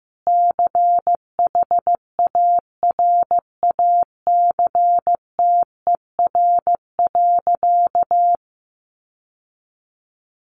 Morse code, heard as a keyed tone, about 15 wpm, 700 Hz. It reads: CHARACTER.
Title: Sentences from Top 1200 Words 15wpm